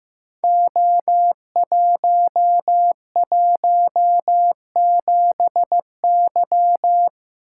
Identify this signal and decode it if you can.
Morse code, heard as a keyed tone, about 15 wpm, 700 Hz. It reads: O117Y
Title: Code Groups - 5 Letter:Numbers - 15wpm